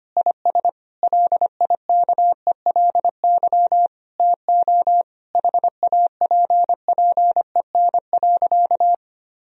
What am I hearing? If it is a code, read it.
Morse code, heard as a keyed tone, about 25 wpm, 700 Hz. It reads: IS LIKELY TO HAPPEN.
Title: Sentences from Top 1000 Words - Review Part 4 - 25wpm